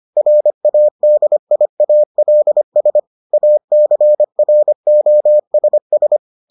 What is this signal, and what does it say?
Morse code, heard as a keyed tone, about 25 wpm, 600 Hz. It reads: RADIALS ACROSS